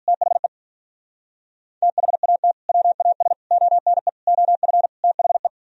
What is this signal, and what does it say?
Morse code, heard as a keyed tone, about 42 wpm, 700 Hz. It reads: THE THAT WAS ONE OF THE
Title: Speed-Racing - Sets of 5 Words 35wpm